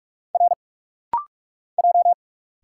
Morse code, heard as a keyed tone, about 46 wpm, 690 Hz. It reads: R J
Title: Single Letters - Rapid-Fire 45wpm